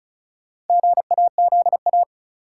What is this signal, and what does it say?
Morse code, heard as a keyed tone, about 36 wpm, 700 Hz. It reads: GAZA
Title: Sets of 5 Words - New Word Drill - 35wpm